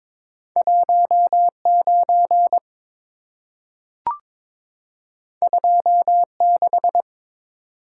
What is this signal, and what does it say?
Morse code, heard as a keyed tone, about 23 wpm, 690 Hz. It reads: 19 26
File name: practice / 22wpm / Two Letter-Number Combinations - 22wpm